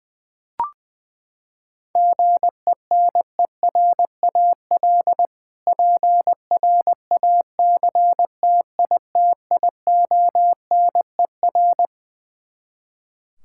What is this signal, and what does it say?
Morse code, heard as a keyed tone, about 21 wpm, 700 Hz. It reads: GENERAL PRACTITIONER
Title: Practice QSO - Exchange 3 - More Info - 20wpm